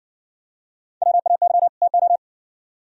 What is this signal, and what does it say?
Morse code, heard as a keyed tone, about 58 wpm, 700 Hz. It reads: WAY TO